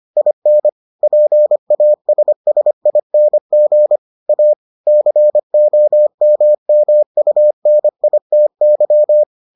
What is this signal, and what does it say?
Morse code, heard as a keyed tone, about 25 wpm, 600 Hz. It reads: IN PASSING A COMMUNITY